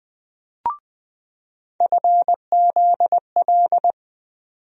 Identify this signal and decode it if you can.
Morse code, heard as a keyed tone, about 20 wpm, 700 Hz. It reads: FZL
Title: USA Call Sign Postfixes - 3 Letters 20wpm